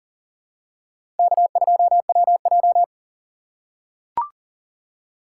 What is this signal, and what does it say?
Morse code, heard as a keyed tone, about 37 wpm, 700 Hz. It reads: K2WJ
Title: USA Call Signs - 4 Characters - 3x Character Spacing - 40wpm